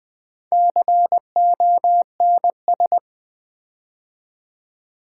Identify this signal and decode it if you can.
Morse code, heard as a keyed tone, about 21 wpm, 700 Hz. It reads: CONS
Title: Build A Word - 4 Letters At A Time - 12-Letter Words - 20wpm